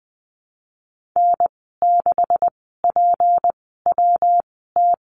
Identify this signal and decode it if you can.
Morse code, heard as a keyed tone, about 20 wpm, spaced as for 15 wpm, 700 Hz. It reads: N6PWT
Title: 173 Lesson 28g - Number 9 - Call Signs 20wpm